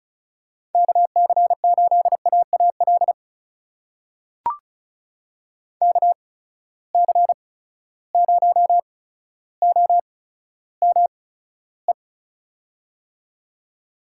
Morse code, heard as a keyed tone, about 36 wpm, spaced as for 15 wpm, 700 Hz. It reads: KC8AAL KC0OME
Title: USA Call Signs - 6 Characters - 8x Character Spacing - 35wpm